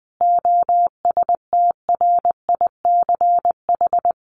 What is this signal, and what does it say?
Morse code, heard as a keyed tone, about 20 wpm, 700 Hz. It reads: OSTRICH
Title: A Sound Of Thunder 20wpm